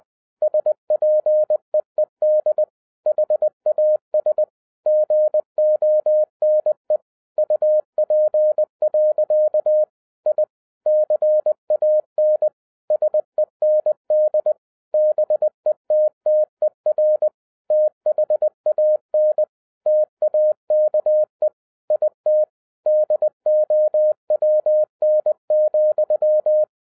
Morse code, heard as a keyed tone, about 20 wpm, 610 Hz. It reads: SPEED HAS GONE UP. I CAN SEND BETTER THAN TAKE IT DOWN,
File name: 2011-07-mp3 / n08 A LOT HAS CHANGED 20wpm